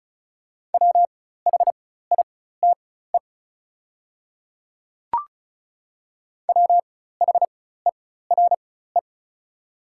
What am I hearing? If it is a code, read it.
Morse code, heard as a keyed tone, about 36 wpm, spaced as for 16 wpm, 690 Hz. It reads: WHITE WHERE
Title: 4x Character Spacing - Top 500 Words - 35wpm